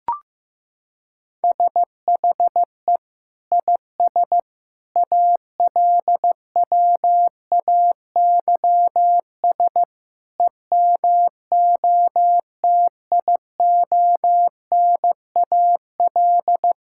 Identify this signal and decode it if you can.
Morse code, heard as a keyed tone, about 15 wpm, 700 Hz. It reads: SHE IS ALWAYS EMOTIONAL
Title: Sentences from Top 1000 Words - Review Part 5 - 15wpm